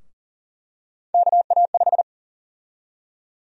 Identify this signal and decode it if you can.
Morse code, heard as a keyed tone, about 37 wpm, 700 Hz. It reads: KA5
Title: USA Call Sign Prefixes - Repeated 2x - 40wpm